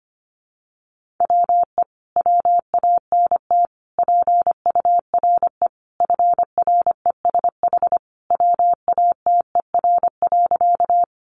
Morse code, heard as a keyed tone, about 25 wpm, 700 Hz. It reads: WE WANT PURE FRESH WATER.
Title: Sentences from Top 1500 Words 25wpm